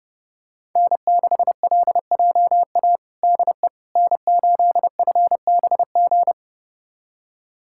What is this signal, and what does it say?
Morse code, heard as a keyed tone, about 30 wpm, 700 Hz. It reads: N6LJA DE N8FBG
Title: USA Callsign DE Callsign - 5 Characters - 30wpm